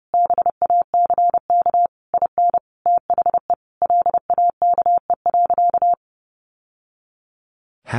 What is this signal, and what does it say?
Morse code, heard as a keyed tone, about 28 wpm, 700 Hz. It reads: BACK IN THE LAKE.
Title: Sentences from Top 1100 Words 30wpm